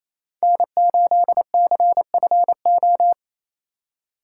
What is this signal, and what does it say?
Morse code, heard as a keyed tone, about 28 wpm, 700 Hz. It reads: N8CFO
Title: USA Call Signs - 5 Characters - 2x Character Spacing - 28wpm